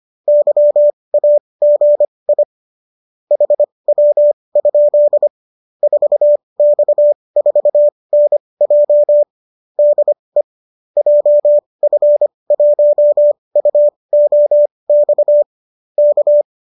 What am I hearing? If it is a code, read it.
Morse code, heard as a keyed tone, about 25 wpm, spaced as for 20 wpm, 590 Hz. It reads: YAGI HW? 4X4NJ DE JF1UOX K